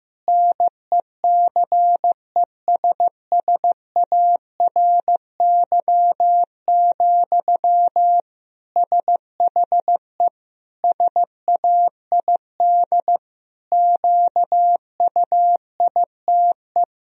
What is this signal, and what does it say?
Morse code, heard as a keyed tone, about 15 wpm, 700 Hz. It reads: NECESSARY, SHE SAID QUITE
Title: Sentences from Top 1000 Words - Review Part 4 - 15wpm